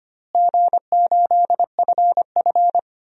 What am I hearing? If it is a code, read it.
Morse code, heard as a keyed tone, about 25 wpm, 700 Hz. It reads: G8FF
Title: DXCC-100 Call Signs - 4 Characters - Repeated 2x - 25wpm